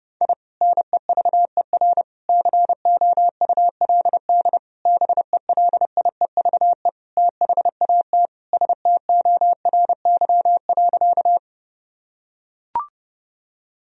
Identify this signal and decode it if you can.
Morse code, heard as a keyed tone, about 30 wpm, 700 Hz. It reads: I NEVER COULD BELIEVE THAT STORY.
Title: Sentences from Top 1000 Words - Review Part 1 - 30wpm